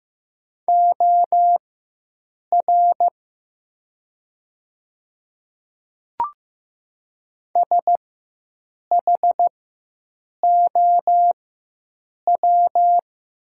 Morse code, heard as a keyed tone, about 15 wpm, spaced as for 7 wpm, 700 Hz. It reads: OR SHOW
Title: 4x Character Spacing - Top 300 Words - 15wpm